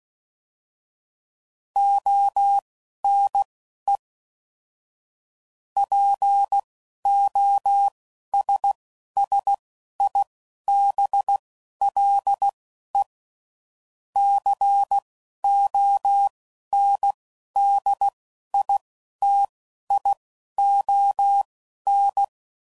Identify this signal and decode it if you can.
Morse code, heard as a keyed tone, about 16 wpm, spaced as for 11 wpm, 800 Hz. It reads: ONE POSSIBLE CONDITION
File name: CW Code Groups @10WPM/Technical Subjects